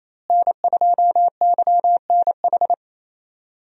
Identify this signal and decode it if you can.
Morse code, heard as a keyed tone, about 28 wpm, 700 Hz. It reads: N2YNH
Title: USA Call Signs - 5 Characters - Repeated 3x - 28wpm